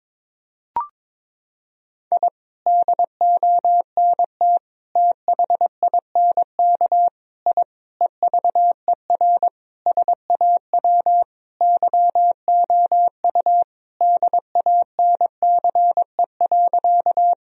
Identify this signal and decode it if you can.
Morse code, heard as a keyed tone, about 22 wpm, 700 Hz. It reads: I DONT THINK I EVER SAW YOU DANCE.
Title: Sentences from Top 1000 Words - Review Part 4 - 22wpm